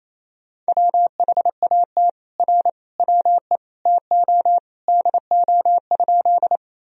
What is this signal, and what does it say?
Morse code, heard as a keyed tone, about 28 wpm, 700 Hz. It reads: WHAT R WE TO DO?
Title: Sentences from Top 100 Words and Abbreviations 28wpm